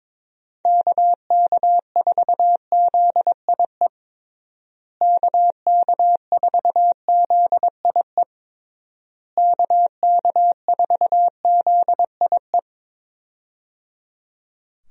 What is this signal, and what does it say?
Morse code, heard as a keyed tone, about 23 wpm, 700 Hz. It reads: KK4ZIE KK4ZIE KK4ZIE
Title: USA Call Signs - 6 Characters - Repeated 3x - 22wpm